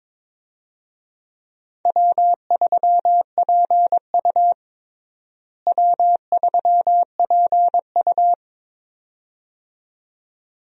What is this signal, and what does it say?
Morse code, heard as a keyed tone, about 22 wpm, 700 Hz. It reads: W3PU W3PU
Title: USA Call Signs - 4 Characters - Repeated 2x - 22wpm